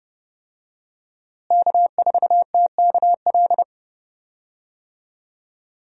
Morse code, heard as a keyed tone, about 30 wpm, 700 Hz. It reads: K4TKL
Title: USA Call Signs - 5 Characters - Repeated 2x - 30wpm